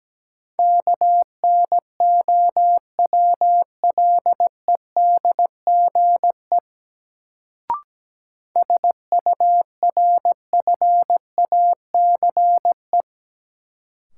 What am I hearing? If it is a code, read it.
Morse code, heard as a keyed tone, about 17 wpm, 700 Hz. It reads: KNOWLEDGE SURFACE
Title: Sentences from Top 600 Words - New Word Drill - 17wpm